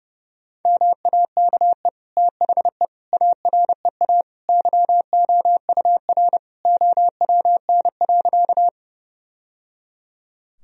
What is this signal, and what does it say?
Morse code, heard as a keyed tone, about 32 wpm, 700 Hz. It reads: MAKE THE AREA YOUR OWN.
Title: Sentences from Top 1000 Words - Review Part 2 - 30wpm